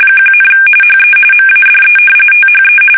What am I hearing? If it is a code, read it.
RTTY, about 45.45 baud. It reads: 1, HE MADE THE
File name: fsk850 / battleshipnj